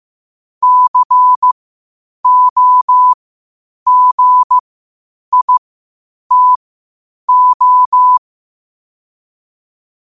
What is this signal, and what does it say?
Morse code, heard as a keyed tone, about 15 wpm, spaced as for 8 wpm, 1000 Hz. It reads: COGITO
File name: qotdfiles / 8wpm